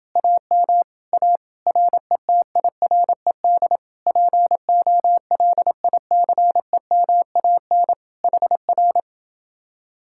Morse code, heard as a keyed tone, about 32 wpm, 700 Hz. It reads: AM A RETIRED POLICEMAN HR